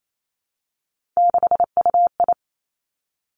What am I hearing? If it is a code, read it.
Morse code, heard as a keyed tone, about 28 wpm, 700 Hz. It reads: 6UI
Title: Three Challenging Letter-Number Combinations 28wpm